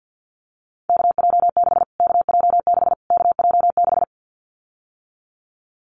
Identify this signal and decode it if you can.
Morse code, heard as a keyed tone, about 50 wpm, 700 Hz. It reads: KJ6 KJ6 KJ6
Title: Callsign Prefixes for Top 25 DXCC - Repeated 3x - 50wpm